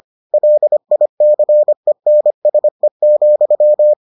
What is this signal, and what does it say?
Morse code, heard as a keyed tone, about 25 wpm, 600 Hz. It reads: LICENSE,